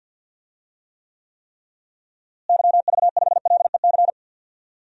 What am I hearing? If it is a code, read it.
Morse code, heard as a keyed tone, about 52 wpm, 700 Hz. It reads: YV5LEC
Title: DXCC-100 Call Signs - 6 Characters - Repeated 3x - 50wpm